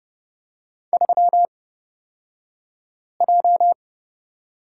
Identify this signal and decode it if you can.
Morse code, heard as a keyed tone, about 30 wpm, spaced as for 5 wpm, 700 Hz. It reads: 3J
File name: Lesson 36d - BT - Two Characters 30wpm